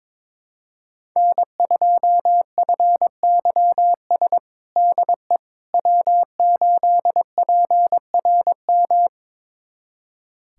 Morse code, heard as a keyed tone, about 22 wpm, 700 Hz. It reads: N2FYS DE W8PRM
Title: USA Callsign DE Callsign - 5 Characters - 22wpm